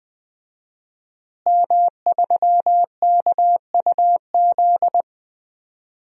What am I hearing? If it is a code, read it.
Morse code, heard as a keyed tone, about 20 wpm, 700 Hz. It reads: M3KUZ
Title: DXCC-100 Call Signs - 5 Characters - Repeated 2x - 20wpm